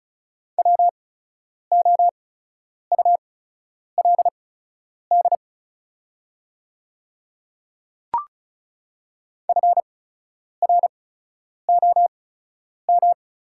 Morse code, heard as a keyed tone, about 36 wpm, spaced as for 10 wpm, 700 Hz. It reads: WOULD FROM
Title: 8x Character Spacing - Top 100 Words - 35wpm